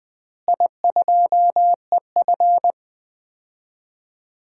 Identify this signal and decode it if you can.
Morse code, heard as a keyed tone, about 20 wpm, 700 Hz. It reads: I2EF
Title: DXCC-100 Call Signs - 4 Characters - Repeated 3x - 20wpm